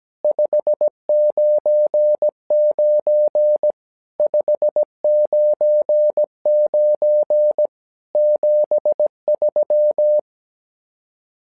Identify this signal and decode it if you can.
Morse code, heard as a keyed tone, about 17 wpm, 600 Hz. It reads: 599 599 73
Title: SOTA QSO Practice 17wpm